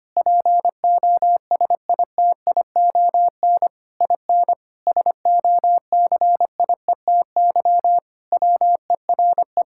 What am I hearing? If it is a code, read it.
Morse code, heard as a keyed tone, about 25 wpm, 700 Hz. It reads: POSITION IN SOCIETY WERE
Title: Sentences from Top 1000 Words - Review Part 5 - 25wpm